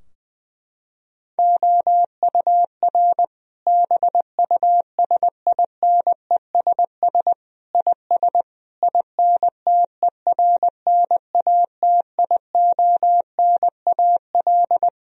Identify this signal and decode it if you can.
Morse code, heard as a keyed tone, about 20 wpm, 700 Hz. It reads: OUR BUSINESS IS INTERNATIONAL
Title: Sentences from Top 1000 Words - Review Part 4 - 20wpm